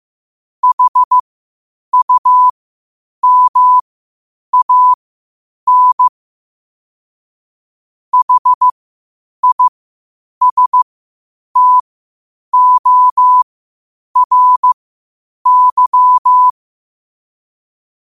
Morse code, heard as a keyed tone, about 15 wpm, spaced as for 8 wpm, 1000 Hz. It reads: HUMAN HISTORY
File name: qotdfiles / 8wpm